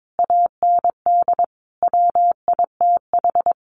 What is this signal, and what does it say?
Morse code, heard as a keyed tone, about 22 wpm, 700 Hz. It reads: AND WITH
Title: Sentences from Top 900 Words 22wpm